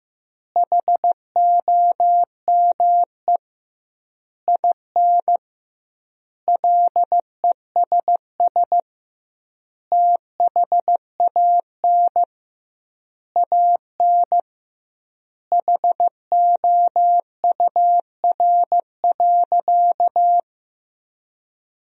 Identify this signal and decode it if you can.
Morse code, heard as a keyed tone, about 15 wpm, 700 Hz. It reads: HOME IN LESS THAN AN HOUR.
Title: 2x Word-Spacing - Sentences from Top 500 Words - 15wpm